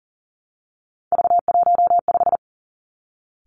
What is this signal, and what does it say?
Morse code, heard as a keyed tone, about 39 wpm, 700 Hz. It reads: V15
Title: Three Challenging Letter-Number Combinations 40wpm